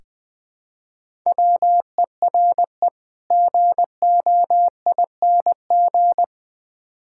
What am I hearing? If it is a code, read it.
Morse code, heard as a keyed tone, about 20 wpm, 700 Hz. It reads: WERE GOING
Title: Build A Phrase - 3-Word Phrases - 20wpm